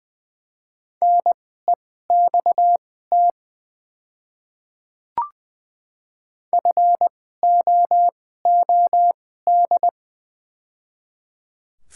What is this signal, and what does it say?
Morse code, heard as a keyed tone, about 20 wpm, spaced as for 14 wpm, 700 Hz. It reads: NEXT FOOD
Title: 2x Character Spacing - Top 300 Words - 20wpm